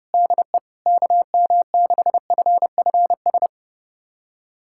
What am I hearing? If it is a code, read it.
Morse code, heard as a keyed tone, about 30 wpm, 700 Hz. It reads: DE KM6FFS
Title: USA Callsign DE Callsign 30wpm